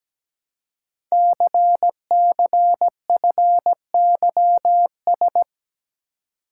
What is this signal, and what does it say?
Morse code, heard as a keyed tone, about 17 wpm, 700 Hz. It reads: CCFYS
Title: Code Groups - 5 Letters - 17wpm